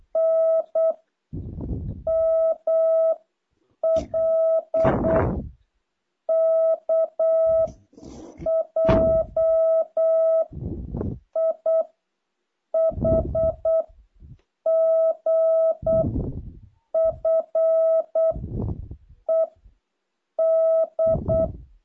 Morse code, heard as a keyed tone, about 8 wpm, 660 Hz. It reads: N M L K J I H G F E D
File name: jh8cbh / 2023cw